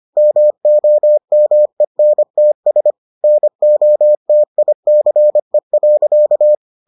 Morse code, heard as a keyed tone, about 25 wpm, 600 Hz. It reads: MOMENTS NOTICE.